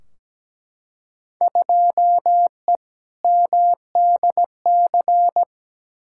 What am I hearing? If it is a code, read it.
Morse code, heard as a keyed tone, about 17 wpm, 700 Hz. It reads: 2E MDC
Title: Field Day Exchanges - 17wpm